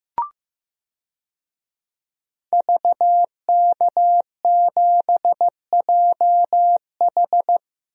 Morse code, heard as a keyed tone, about 16 wpm, 700 Hz. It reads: VK7JH
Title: DXCC-100 Call Signs - 5 Characters - Repeated 2x - 15wpm